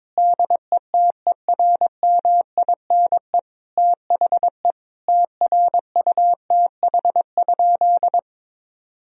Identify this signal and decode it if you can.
Morse code, heard as a keyed tone, about 22 wpm, 700 Hz. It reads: DETERMINE THE TRUTH?
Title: Sentences from Top 1000 Words - Review Part 4 - 22wpm